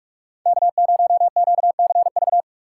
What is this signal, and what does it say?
Morse code, heard as a keyed tone, about 42 wpm, 700 Hz. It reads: K0QCV
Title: USA Call Signs - 2x Character Spacing - 45wpm